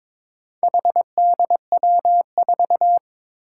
Morse code, heard as a keyed tone, about 22 wpm, 700 Hz. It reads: HDW4
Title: Four Challenging Letter-Number Combinations 22wpm